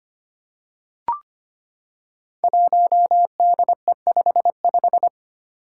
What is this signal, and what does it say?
Morse code, heard as a keyed tone, about 25 wpm, 700 Hz. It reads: 1DE55
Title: Five Challenging Letter-Number Combinations 25wpm